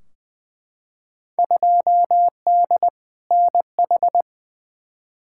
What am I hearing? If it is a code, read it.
Morse code, heard as a keyed tone, about 20 wpm, 700 Hz. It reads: 2D NH
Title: Field Day Exchanges Encore - 20wpm